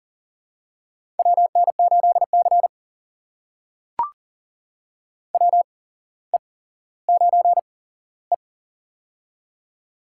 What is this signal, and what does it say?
Morse code, heard as a keyed tone, about 40 wpm, spaced as for 17 wpm, 700 Hz. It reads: WN8C WE9E
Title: USA Call Signs - 4 Characters - 8x Character Spacing - 40wpm